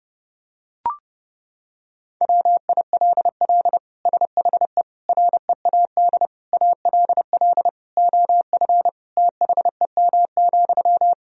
Morse code, heard as a keyed tone, about 30 wpm, 700 Hz. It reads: WILL SHE READ ALL OF THEM,
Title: Sentences from Top 1000 Words - Review Part 4 - 30wpm